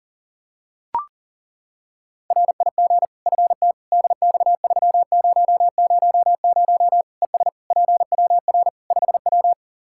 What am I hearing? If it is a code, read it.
Morse code, heard as a keyed tone, about 37 wpm, 700 Hz. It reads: RIG FT DX3000 ES PWR 5W
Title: K1USN SST QSO Practice 40wpm